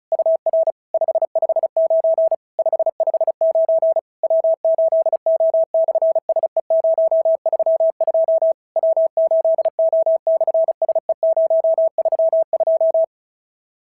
Morse code, heard as a keyed tone, about 36 wpm, 650 Hz. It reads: UR 559 559 W8O/SE032 W8O/SE032
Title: SOTA QSO Practice 35wpm